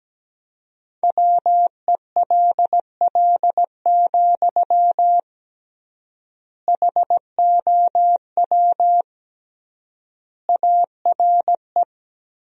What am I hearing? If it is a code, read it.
Morse code, heard as a keyed tone, about 17 wpm, 700 Hz. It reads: WELL, HOW ARE
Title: 3x Word-Spacing - Sentences from Top 200 Words - 17wpm